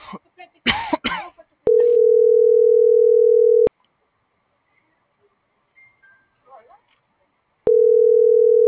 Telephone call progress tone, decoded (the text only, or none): ringback tone